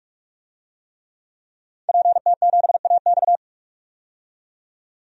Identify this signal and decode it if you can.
Morse code, heard as a keyed tone, about 46 wpm, 700 Hz. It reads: PT7AX
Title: DXCC-100 Call Signs - 5 Characters - 45wpm